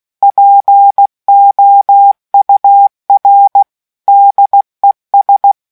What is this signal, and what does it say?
Morse code, heard as a keyed tone, about 16 wpm, 800 Hz. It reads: POUR DES